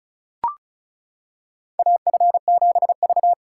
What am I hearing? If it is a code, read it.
Morse code, heard as a keyed tone, about 34 wpm, 700 Hz. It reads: AF7V
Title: USA Callsign DE Callsign - 4 Characters - 35wpm